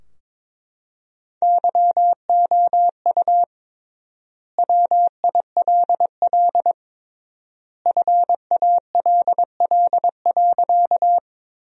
Morse code, heard as a keyed tone, about 22 wpm, 700 Hz. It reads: YOU WILL FALL.
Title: 3x Word-Spacing - Sentences from Top 500 Words - 22wpm